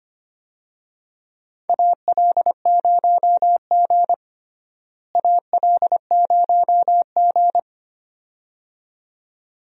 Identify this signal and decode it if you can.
Morse code, heard as a keyed tone, about 25 wpm, 700 Hz. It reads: AL0G AL0G
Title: USA Call Signs - 4 Characters - Repeated 2x - 25wpm